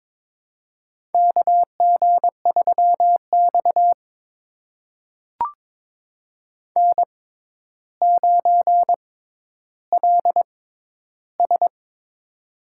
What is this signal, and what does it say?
Morse code, heard as a keyed tone, about 22 wpm, spaced as for 11 wpm, 700 Hz. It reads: KG3X N9LS
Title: USA Call Signs - 4 Characters - 6x Character Spacing - 22wpm